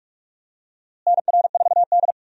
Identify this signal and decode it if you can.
Morse code, heard as a keyed tone, about 46 wpm, 700 Hz. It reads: NR4D
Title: USA Call Signs - 4 Characters - Repeated 2x - 45wpm